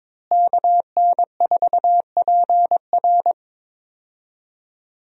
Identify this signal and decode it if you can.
Morse code, heard as a keyed tone, about 21 wpm, 700 Hz. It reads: KN4PR
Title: USA Call Signs - 5 Characters - Repeated 2x - 22wpm